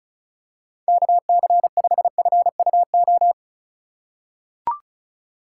Morse code, heard as a keyed tone, about 34 wpm, 700 Hz. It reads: KC5FUO
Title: USA Call Signs - 4x Character Spacing - 35wpm